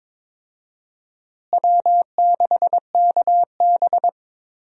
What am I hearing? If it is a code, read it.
Morse code, heard as a keyed tone, about 22 wpm, 700 Hz. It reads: W6KB
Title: USA Call Signs - 4 Characters - Repeated 3x - 22wpm